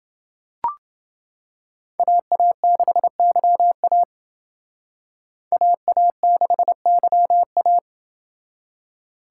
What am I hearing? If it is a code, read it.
Morse code, heard as a keyed tone, about 28 wpm, 700 Hz. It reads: AA6YA AA6YA